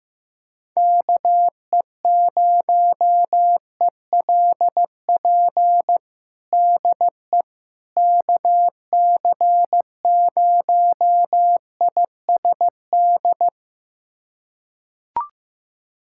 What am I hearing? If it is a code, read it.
Morse code, heard as a keyed tone, about 15 wpm, 700 Hz. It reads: KE0ELP DE KC0ISD
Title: USA Callsign DE Callsign - 6 Characters - 15wpm